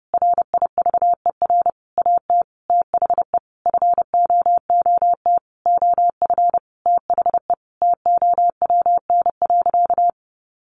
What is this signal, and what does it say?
Morse code, heard as a keyed tone, about 30 wpm, 700 Hz. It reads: RIVER AT THE FOOT OF THE TOWN.
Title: Sentences from Top 1400 Words 30wpm